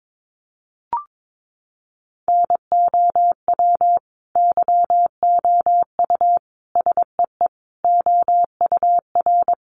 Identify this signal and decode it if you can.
Morse code, heard as a keyed tone, about 22 wpm, 700 Hz. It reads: NOW YOU SEE OUR
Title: Sentences from Top 1500 Words 22wpm